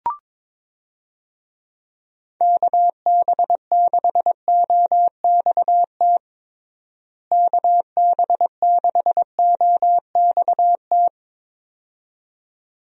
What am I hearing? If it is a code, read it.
Morse code, heard as a keyed tone, about 22 wpm, 700 Hz. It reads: KB6OXT KB6OXT
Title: USA Call Signs - 6 Characters - Repeated 2x - 22wpm